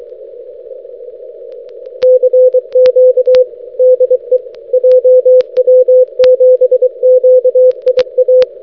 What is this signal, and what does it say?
Morse code, heard as a keyed tone, about 24 wpm, 510 Hz. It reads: CQ DE JW7QIA
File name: jw7qia-6m